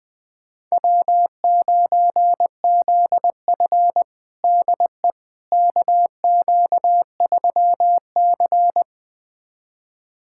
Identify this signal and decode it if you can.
Morse code, heard as a keyed tone, about 21 wpm, 700 Hz. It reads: W9ZF DE KQ3C
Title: USA Callsign DE Callsign - 4 Characters - 20wpm